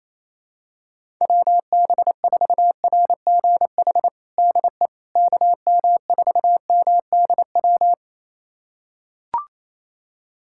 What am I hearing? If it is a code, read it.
Morse code, heard as a keyed tone, about 28 wpm, 700 Hz. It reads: WB4RGH DE KM4MDW
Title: USA Callsign DE Callsign 28wpm